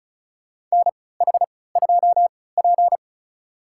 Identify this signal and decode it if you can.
Morse code, heard as a keyed tone, about 34 wpm, spaced as for 20 wpm, 700 Hz. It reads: NH2P
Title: USA Call Signs - 4 Characters - 3x Character Spacing - 35wpm